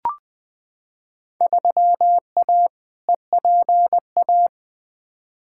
Morse code, heard as a keyed tone, about 20 wpm, 700 Hz. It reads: 3A EPA
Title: Field Day Exchanges - 20wpm